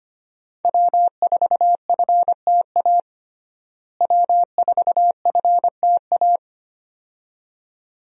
Morse code, heard as a keyed tone, about 25 wpm, 700 Hz. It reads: W4FTA W4FTA
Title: USA Call Signs - 5 Characters - Repeated 2x - 25wpm